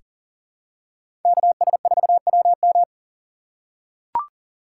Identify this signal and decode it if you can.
Morse code, heard as a keyed tone, about 37 wpm, 700 Hz. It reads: KS4WM
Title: USA Call Signs - 5 Characters - 2x Character Spacing - 40wpm